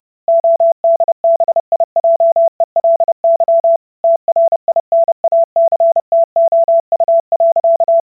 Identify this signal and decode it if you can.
Morse code, heard as a keyed tone, about 30 wpm, 670 Hz. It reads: ODBIJELY TRINACTOU.